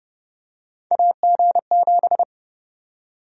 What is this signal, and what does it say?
Morse code, heard as a keyed tone, about 30 wpm, 700 Hz. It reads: AG7
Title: USA Call Sign Prefixes - Repeated 2x - 30wpm